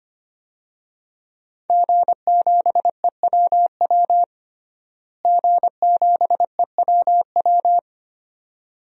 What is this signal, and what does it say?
Morse code, heard as a keyed tone, about 25 wpm, 700 Hz. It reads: G7EWW G7EWW
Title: DXCC-100 Call Signs - 5 Characters - Repeated 2x - 25wpm